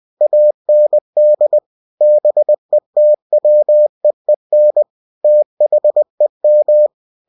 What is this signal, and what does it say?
Morse code, heard as a keyed tone, about 20 wpm, 600 Hz. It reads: AND BETWEEN THEM
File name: cw-books / winnie-the-pooh / winnie-20wpm-page-0106